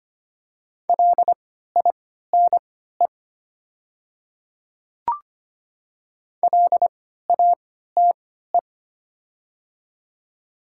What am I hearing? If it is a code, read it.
Morse code, heard as a keyed tone, about 25 wpm, spaced as for 14 wpm, 690 Hz. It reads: LINE LATE
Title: 3x Character Spacing - Top 500 Words - 25wpm